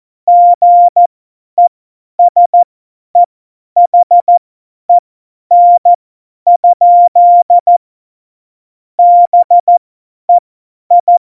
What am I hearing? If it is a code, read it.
Morse code, heard as a keyed tone, about 14 wpm, spaced as for 10 wpm, 700 Hz. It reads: GESEHEN? BEI
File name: Morsekurs / Witze / Witz 02 Juni 2020, 50 F70